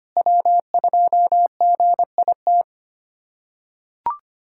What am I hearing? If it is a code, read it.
Morse code, heard as a keyed tone, about 25 wpm, 700 Hz. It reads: W2GIT
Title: USA Callsign DE Callsign - 5 Characters - 25wpm